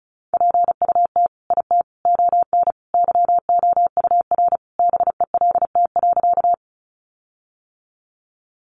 Morse code, heard as a keyed tone, about 34 wpm, 700 Hz. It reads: PUT IT ON YOUR BELT.
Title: Sentences from Top 1400 Words 35wpm